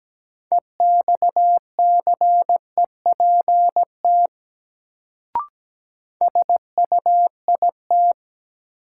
Morse code, heard as a keyed tone, about 17 wpm, 700 Hz. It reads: EXCEPT SUIT
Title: Sentences from Top 1600 Words - New Word Drill Alt - 17wpm